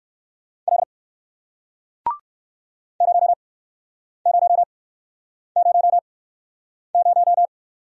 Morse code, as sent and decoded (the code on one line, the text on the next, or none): .-.. / ----- ----- ----- -----
L 0000